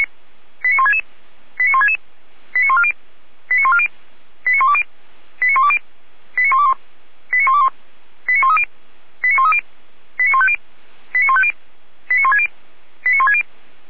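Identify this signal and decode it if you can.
Five-tone selective call (ZVEI), heard as quick runs of five tones, tones of about 70 ms. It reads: 89166 89166 89140 89140 89120 89120 89121 89121 89130 89130 89150 89150 89160 89160
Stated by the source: ZVEI five-tone selective call received with an SDR.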